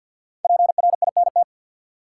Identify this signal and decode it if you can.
Morse code, heard as a keyed tone, about 51 wpm, 700 Hz. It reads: PRINT
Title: Sentences from Top 1400 Words - New Word Drill Alt - 50wpm